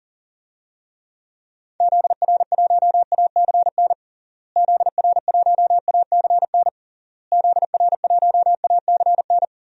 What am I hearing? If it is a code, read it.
Morse code, heard as a keyed tone, about 40 wpm, 700 Hz. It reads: ZR1ACN ZR1ACN ZR1ACN
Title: DXCC-100 Call Signs - 6 Characters - Repeated 3x - 40wpm